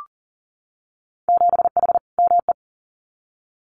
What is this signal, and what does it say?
Morse code, heard as a keyed tone, about 42 wpm, 700 Hz. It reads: 7H ME
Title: Winter Field Day Exchanges Encore 40wpm